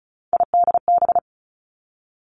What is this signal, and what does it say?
Morse code, heard as a keyed tone, about 36 wpm, 700 Hz. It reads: IDB
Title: Three Challenging Letter-Number Combinations 35wpm